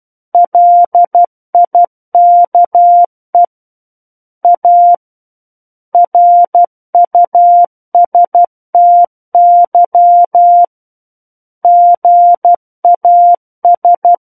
Morse code, heard as a keyed tone, about 12 wpm, 700 Hz. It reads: LIKE A RUSTY GAS